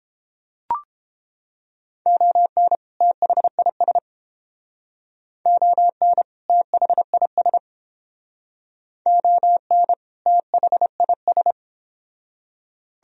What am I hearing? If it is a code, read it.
Morse code, heard as a keyed tone, about 28 wpm, 700 Hz. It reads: ON THIS ON THIS ON THIS